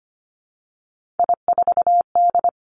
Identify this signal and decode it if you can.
Morse code, heard as a keyed tone, about 25 wpm, 700 Hz. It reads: I4D
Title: Three Challenging Letter-Number Combinations 25wpm